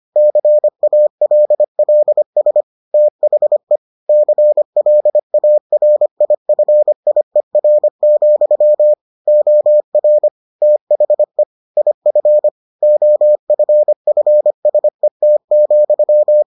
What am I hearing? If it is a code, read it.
Morse code, heard as a keyed tone, about 25 wpm, 600 Hz. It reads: CALLS THE CLARIFIER, OR THE IF OFFSET,